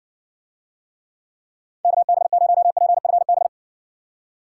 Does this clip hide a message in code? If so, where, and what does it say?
Yes: Morse code from 1.8 to 3.5 s, heard as a keyed tone, about 58 wpm, 700 Hz. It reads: KB0PFB